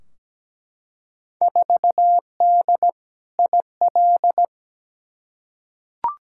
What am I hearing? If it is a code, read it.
Morse code, heard as a keyed tone, about 17 wpm, 700 Hz. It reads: 4D IL
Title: Field Day Exchanges Encore - 17wpm